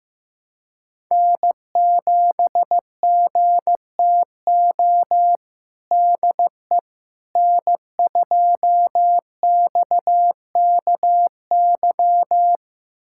Morse code, heard as a keyed tone, about 15 wpm, 700 Hz. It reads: N7GTO DE N2XKY
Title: USA Callsign DE Callsign - 5 Characters - 15wpm